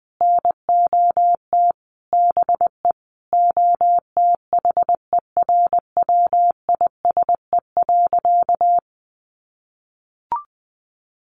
Morse code, heard as a keyed tone, about 20 wpm, 700 Hz. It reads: NOT BE OTHERWISE.
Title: Sentences from Top 1200 Words 20wpm